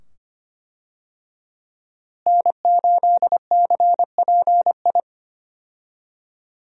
Morse code, heard as a keyed tone, about 25 wpm, 700 Hz. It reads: N8CPI
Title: USA Call Signs - 5 Characters - Repeated 2x - 25wpm